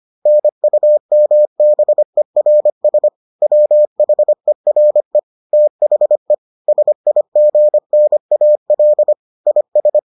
Morse code, heard as a keyed tone, about 25 wpm, 600 Hz. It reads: NUMBERS WHERE THE SIGNAL IS